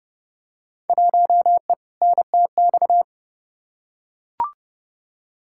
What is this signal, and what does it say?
Morse code, heard as a keyed tone, about 30 wpm, 700 Hz. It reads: 1E NTX
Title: Field Day Exchanges - 30wpm